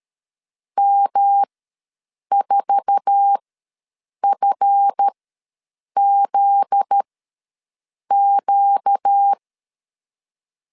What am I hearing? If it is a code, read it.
Morse code, heard as a keyed tone, about 13 wpm, spaced as for 7 wpm, 800 Hz. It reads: M4FZQ
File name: LEZIONI / 8 YJ45 / 8 YJ4 ws 12wpm 3249